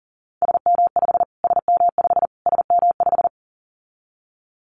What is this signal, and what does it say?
Morse code, heard as a keyed tone, about 37 wpm, 700 Hz. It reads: SM5 SM5 SM5
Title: Callsign Prefixes for Top 25 DXCC - Repeated 3x - 40wpm